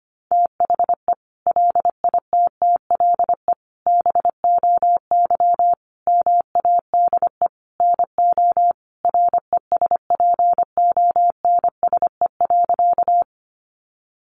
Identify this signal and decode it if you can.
Morse code, heard as a keyed tone, about 25 wpm, 700 Hz. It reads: THE LITTLE BOY MADE NO RESPONSE.
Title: Sentences from Top 1400 Words 25wpm